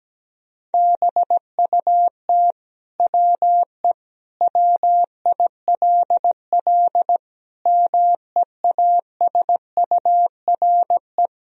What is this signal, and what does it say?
Morse code, heard as a keyed tone, about 17 wpm, 700 Hz. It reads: BUT WE WILL MEASURE
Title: Sentences from Top 1000 Words - Review Part 2 - 17wpm